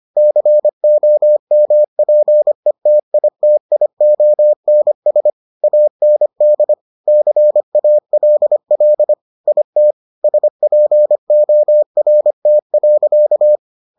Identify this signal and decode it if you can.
Morse code, heard as a keyed tone, about 25 wpm, 600 Hz. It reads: COMPETITIONS AND CALL IT SPORT.